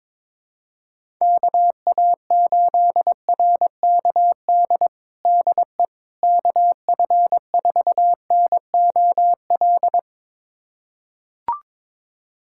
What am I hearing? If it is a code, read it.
Morse code, heard as a keyed tone, about 21 wpm, 700 Hz. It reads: KA8RKD DE KF4NOL
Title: USA Callsign DE Callsign - 6 Characters - 22wpm